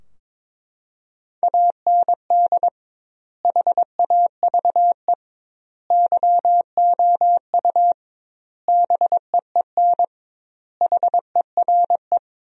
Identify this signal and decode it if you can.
Morse code, heard as a keyed tone, about 22 wpm, 700 Hz. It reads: AND HAVE YOU BEEN HERE
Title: 2x Word-Spacing - Sentences from Top 200 Words - 22wpm